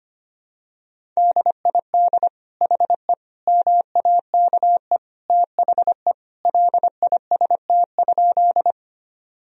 Morse code, heard as a keyed tone, about 25 wpm, 700 Hz. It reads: DID HE MAKE THE LIST?
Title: Sentences from Top 1000 Words - Review Part 2 - 25wpm